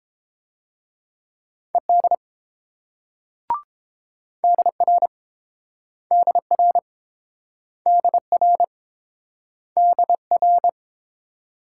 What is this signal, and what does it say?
Morse code, heard as a keyed tone, about 28 wpm, 700 Hz. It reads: ED DR DR DR DR